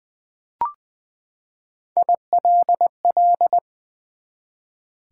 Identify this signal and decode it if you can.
Morse code, heard as a keyed tone, about 20 wpm, 700 Hz. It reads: ILL
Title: Three Letter Words - 20wpm